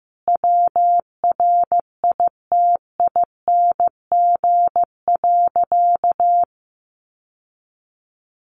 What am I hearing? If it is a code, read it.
Morse code, heard as a keyed tone, about 15 wpm, 700 Hz. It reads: WRITING.
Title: Sentences from Top 800 Words 15wpm